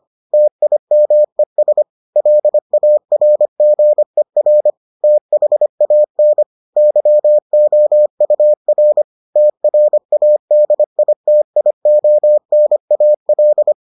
Morse code, heard as a keyed tone, about 25 wpm, 600 Hz. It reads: TIMES LARGER THAN YOUR TRADITIONAL